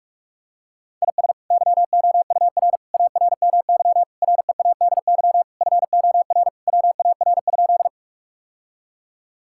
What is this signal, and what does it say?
Morse code, heard as a keyed tone, about 46 wpm, 700 Hz. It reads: IS YOUR ARMY READY FOR WAR?
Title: Sentences from Top 1000 Words - Review Part 5 - 45wpm